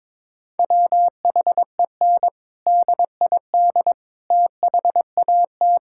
Morse code, heard as a keyed tone, about 21 wpm, 700 Hz. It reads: WHEN DID THAT
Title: Sentences from Top 1000 Words - Review Part 6 - 22wpm